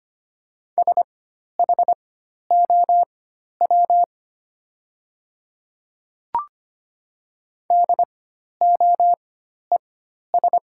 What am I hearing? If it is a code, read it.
Morse code, heard as a keyed tone, about 25 wpm, spaced as for 12 wpm, 700 Hz. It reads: SHOW DOES
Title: 4x Character Spacing - Top 200 Words - 25wpm